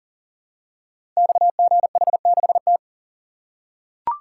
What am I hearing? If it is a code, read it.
Morse code, heard as a keyed tone, about 43 wpm, 690 Hz. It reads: XGH6T
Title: Code Groups - 5 Letter:Numbers - 40wpm